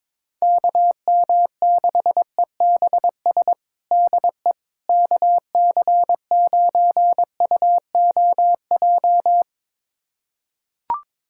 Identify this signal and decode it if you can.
Morse code, heard as a keyed tone, about 22 wpm, 700 Hz. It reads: KM6EBS DE KC9UOJ
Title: USA Callsign DE Callsign - 6 Characters - 22wpm